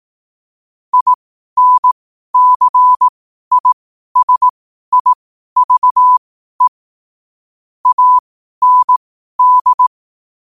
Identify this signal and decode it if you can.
Morse code, heard as a keyed tone, about 18 wpm, spaced as for 12 wpm, 1000 Hz. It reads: INCISIVE AND
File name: qotdfiles / 12wpm